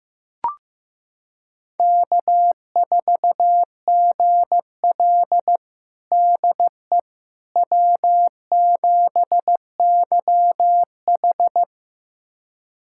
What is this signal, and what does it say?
Morse code, heard as a keyed tone, about 15 wpm, 700 Hz. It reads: K4GL DE W7YH
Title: USA Callsign DE Callsign - 4 Characters - 15wpm